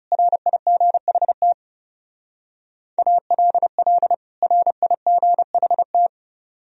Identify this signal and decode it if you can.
Morse code, heard as a keyed tone, about 35 wpm, 700 Hz. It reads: RIGHT ALL RIGHT
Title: Speed-Racing - Sets of 2 Words 30wpm